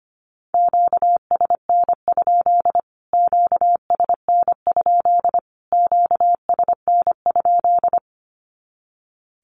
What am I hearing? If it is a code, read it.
Morse code, heard as a keyed tone, about 25 wpm, 700 Hz. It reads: QSN? QSN? QSN?
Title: Q-Codes Repeated 3x 25wpm